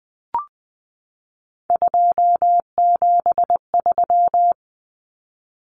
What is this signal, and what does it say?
Morse code, heard as a keyed tone, about 20 wpm, 700 Hz. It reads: 273
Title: Common Cut Numbers - 3-Digit 20wpm